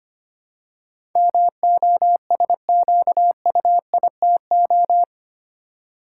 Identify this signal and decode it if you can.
Morse code, heard as a keyed tone, about 25 wpm, 700 Hz. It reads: MOSQUITO
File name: 8 Letter Words - First 3 Letters Given - 25wpm